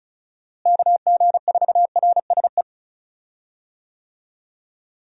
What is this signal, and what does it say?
Morse code, heard as a keyed tone, about 34 wpm, 700 Hz. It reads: KG4RSE
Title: USA Call Signs - 6 Characters - Repeated 2x - 35wpm